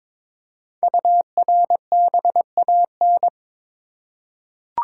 Morse code, heard as a keyed tone, about 22 wpm, 700 Hz. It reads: URBAN
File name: 5 Letter Words - First 3 Letters Given - 22wpm